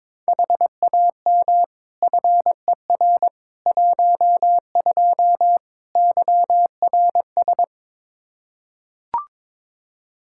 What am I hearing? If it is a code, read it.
Morse code, heard as a keyed tone, about 22 wpm, 700 Hz. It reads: HAM FER 12 YRS
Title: Practice QSO - Exchange 3 - More Info - 22wpm